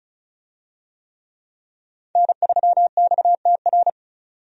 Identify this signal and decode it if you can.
Morse code, heard as a keyed tone, about 36 wpm, 700 Hz. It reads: N3XTR
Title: USA Call Signs - 5 Characters - 35wpm